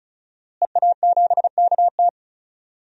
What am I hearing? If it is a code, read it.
Morse code, heard as a keyed tone, about 34 wpm, 700 Hz. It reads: EA7KT
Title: DXCC-100 Call Signs - 5 Characters - Repeated 3x - 35wpm